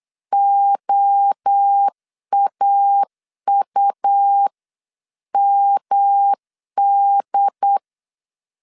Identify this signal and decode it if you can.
Morse code, heard as a keyed tone, about 8 wpm, 800 Hz. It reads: OAUMD